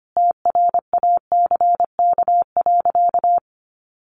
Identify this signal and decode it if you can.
Morse code, heard as a keyed tone, about 25 wpm, 700 Hz. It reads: TRACK.
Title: Sentences from Top 1100 Words 25wpm